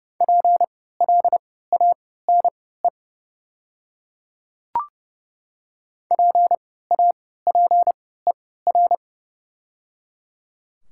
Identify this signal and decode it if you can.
Morse code, heard as a keyed tone, about 30 wpm, spaced as for 17 wpm, 690 Hz. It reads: PLANE PAPER
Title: 3x Character Spacing - Top 500 Words - 30wpm